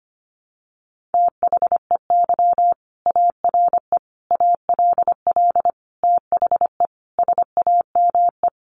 Morse code, heard as a keyed tone, about 25 wpm, 700 Hz. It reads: THEY ARE ALL THE SAME
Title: Sentences from Top 600 Words 25wpm